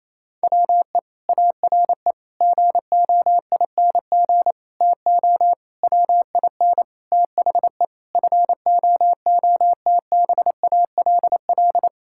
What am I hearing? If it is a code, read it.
Morse code, heard as a keyed tone, about 28 wpm, 700 Hz. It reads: WE ARE GOING TO WIN THE FOOTBALL
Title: Sentences from Top 1600 Words 28wpm